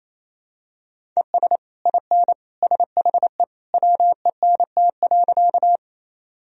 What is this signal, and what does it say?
Morse code, heard as a keyed tone, about 28 wpm, 700 Hz. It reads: ES IN SHE WENT.
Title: Sentences from Top 200 Words and Abbreviations 28wpm